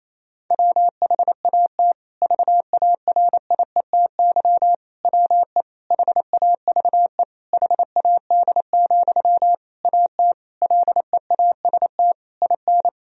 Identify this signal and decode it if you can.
Morse code, heard as a keyed tone, about 28 wpm, 700 Hz. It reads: WHAT VARIETY WE HAVE HAD, AT LEAST IN
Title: Sentences from Top 1000 Words - Review Part 4 - 28wpm